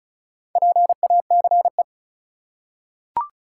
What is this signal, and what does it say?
Morse code, heard as a keyed tone, about 36 wpm, 700 Hz. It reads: PACE
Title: Four Letter Words - 35wpm